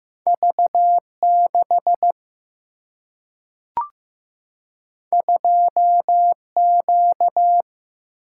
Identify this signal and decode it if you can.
Morse code, heard as a keyed tone, about 15 wpm, 700 Hz. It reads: V 6 2 Q
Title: Single-Letter Single-Number 15wpm